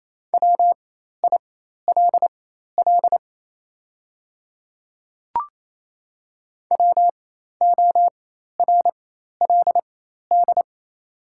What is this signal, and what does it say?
Morse code, heard as a keyed tone, about 28 wpm, spaced as for 13 wpm, 690 Hz. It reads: WILL WORLD
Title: 4x Character Spacing - Top 300 Words - 28wpm